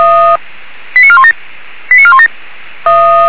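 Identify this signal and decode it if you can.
Five-tone selective call (ZVEI), heard as quick runs of five tones, tones of about 70 ms. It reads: 89417 89417
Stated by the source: ZVEI five-tone selective call received on an SDR, recorded by the feuerwehr-orlamuende.org fire brigade server.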